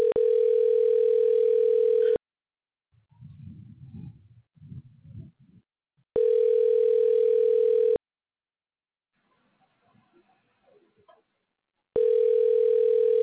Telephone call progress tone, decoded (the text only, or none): ringback tone